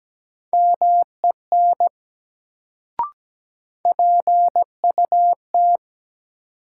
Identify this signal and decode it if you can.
Morse code, heard as a keyed tone, about 17 wpm, 700 Hz. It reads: MEN PUT
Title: Sentences from Top 200 Words - New Word Drill - 17wpm